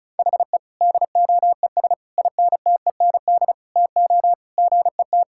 Morse code, heard as a keyed tone, about 34 wpm, 700 Hz. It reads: HE DOES INTEND TO GET
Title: Sentences from Top 1600 Words 35wpm